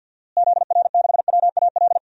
Morse code, heard as a keyed tone, about 50 wpm, 700 Hz. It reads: ZR6WAL
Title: DXCC-100 Call Signs - 6 Characters - Repeated 3x - 50wpm